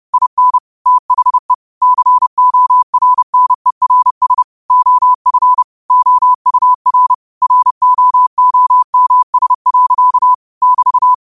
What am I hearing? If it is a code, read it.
Morse code, heard as a keyed tone, about 30 wpm, 1000 Hz. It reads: IN THE CORNERS OF OUR ROOMS. <BT>